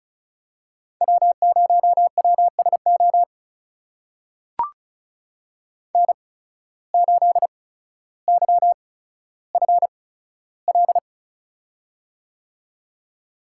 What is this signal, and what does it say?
Morse code, heard as a keyed tone, about 34 wpm, spaced as for 15 wpm, 700 Hz. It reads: W0WSO N8YFL
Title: USA Call Signs - 5 Characters - 8x Character Spacing - 35wpm